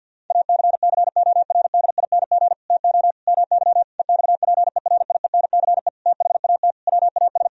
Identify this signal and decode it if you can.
Morse code, heard as a keyed tone, about 50 wpm, 700 Hz. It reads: ACCORDING TO MY EXPERIENCE THAT WAS
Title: Sentences from Top 1000 Words - Review Part 4 - 50wpm